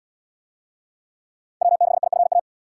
Morse code, heard as a keyed tone, about 77 wpm, 700 Hz. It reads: W6IFA